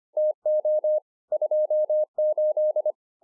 Morse code, heard as a keyed tone, about 24 wpm, 610 Hz. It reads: TO 28